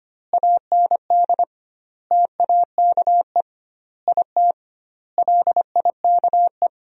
Morse code, heard as a keyed tone, about 25 wpm, 700 Hz. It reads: AND TAKE IT LIKE
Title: 2x Word-Spacing - Sentences from Top 300 Words - 25wpm